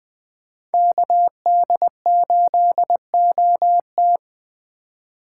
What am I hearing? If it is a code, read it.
Morse code, heard as a keyed tone, about 21 wpm, 700 Hz. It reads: KD8OT
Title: USA Call Signs - 5 Characters - 4x Character Spacing - 20wpm